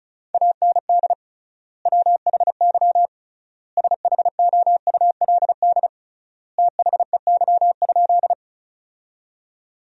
Morse code, heard as a keyed tone, about 36 wpm, 700 Hz. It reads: AND WHY SHOULD THEY?
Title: 3x Word-Spacing - Sentences from Top 300 Words - 35wpm